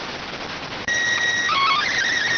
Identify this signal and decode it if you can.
SSTV (slow-scan television), PD 290.